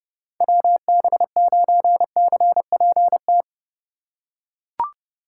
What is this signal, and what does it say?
Morse code, heard as a keyed tone, about 28 wpm, 700 Hz. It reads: WB9CPT